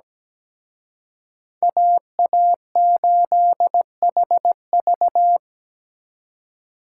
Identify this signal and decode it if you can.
Morse code, heard as a keyed tone, about 20 wpm, 700 Hz. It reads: AA8HV
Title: Speed Racing - USA 2x2 Call Signs 17wpm